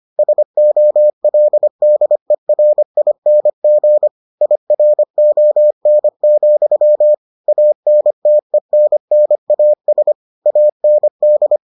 Morse code, heard as a keyed tone, about 25 wpm, 600 Hz. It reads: SOLDERING IRON, ANTENNAS AND